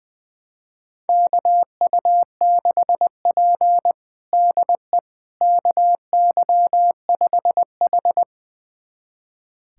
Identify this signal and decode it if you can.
Morse code, heard as a keyed tone, about 21 wpm, 700 Hz. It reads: KU6P DE KY5H
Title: USA Callsign DE Callsign - 4 Characters - 20wpm